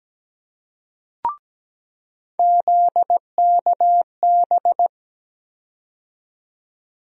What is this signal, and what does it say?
Morse code, heard as a keyed tone, about 17 wpm, 700 Hz. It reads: ZKB
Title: USA Call Sign Postfixes - 3 Letters 17wpm